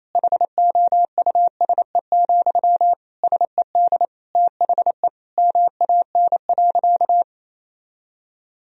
Morse code, heard as a keyed tone, about 28 wpm, 700 Hz. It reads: HOUSE, SED THE MAN.
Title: Sentences from Top 400 Words and Abbreviations 28wpm